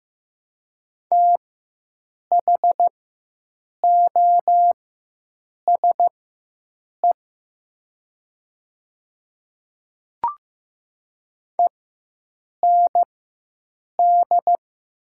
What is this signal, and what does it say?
Morse code, heard as a keyed tone, about 15 wpm, spaced as for 7 wpm, 700 Hz. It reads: THOSE END
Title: 4x Character Spacing - Top 300 Words - 15wpm